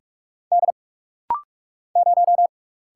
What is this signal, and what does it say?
Morse code, heard as a keyed tone, about 46 wpm, 700 Hz. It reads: D 0
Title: Single Letter-Number Alt - Rapid-Fire 45wpm